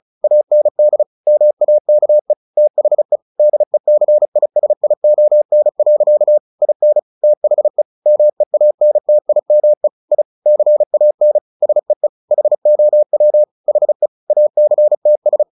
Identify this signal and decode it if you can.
Morse code, heard as a keyed tone, about 34 wpm, 600 Hz. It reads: AND MAKE THE DECISION. IN THE MEANTIME I CAN SEE HOW HE ACTS
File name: the old man and the sea 2 35wpm